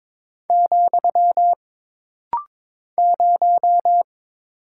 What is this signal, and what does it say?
Morse code, heard as a keyed tone, about 22 wpm, 700 Hz. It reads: , 0
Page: This rapid-fire timing is geared to push your ICR (Instant Character Recognition) skills.